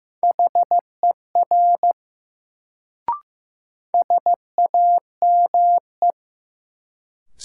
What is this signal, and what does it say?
Morse code, heard as a keyed tone, about 15 wpm, 700 Hz. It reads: HER SAME
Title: Sets of 2 Words - New Word Drill - 15wpm